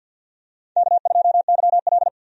Morse code, heard as a keyed tone, about 49 wpm, 700 Hz. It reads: K2YL
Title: USA Call Signs - 4 Characters - 50wpm